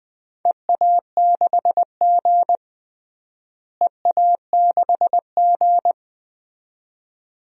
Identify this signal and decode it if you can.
Morse code, heard as a keyed tone, about 20 wpm, 700 Hz. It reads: EA6G EA6G
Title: DXCC-100 Call Signs - 4 Characters - Repeated 2x - 20wpm